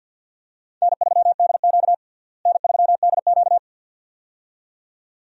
Morse code, heard as a keyed tone, about 51 wpm, 700 Hz. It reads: N3DQ N3DQ
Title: USA Call Signs - 4 Characters - Repeated 2x - 50wpm